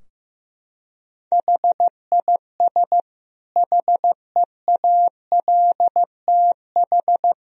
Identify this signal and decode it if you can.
Morse code, heard as a keyed tone, about 16 wpm, 700 Hz. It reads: HIS HEALTH
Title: Sentences from Top 1000 Words - Review Part 3 - 15wpm